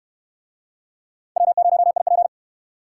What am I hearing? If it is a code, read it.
Morse code, heard as a keyed tone, about 68 wpm, 700 Hz. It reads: W0IP